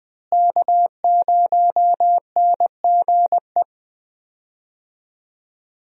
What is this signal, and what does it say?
Morse code, heard as a keyed tone, about 20 wpm, 700 Hz. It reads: K0NGE
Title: USA Call Signs - 5 Characters - Repeated 2x - 20wpm